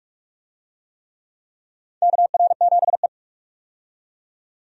Morse code, heard as a keyed tone, about 46 wpm, 700 Hz. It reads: KR7E
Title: USA Call Signs - 4 Characters - 45wpm